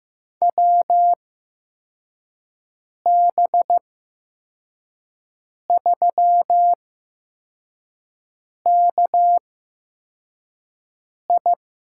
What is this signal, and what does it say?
Morse code, heard as a keyed tone, about 15 wpm, spaced as for 4 wpm, 700 Hz. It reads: WB3KI
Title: USA Call Signs - 6 Characters - 8x Character Spacing - 15wpm